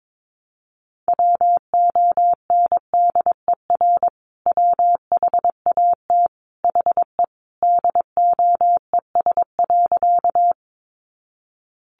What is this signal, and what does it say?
Morse code, heard as a keyed tone, about 22 wpm, 700 Hz. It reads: WONDER WHAT HE DOES.
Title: Sentences from Top 500 Words 22wpm